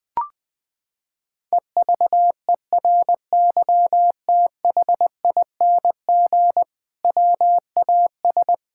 Morse code, heard as a keyed tone, about 19 wpm, 700 Hz. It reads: EVERYTHING WAS
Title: Sentences from Top 1000 Words - Review Part 5 - 20wpm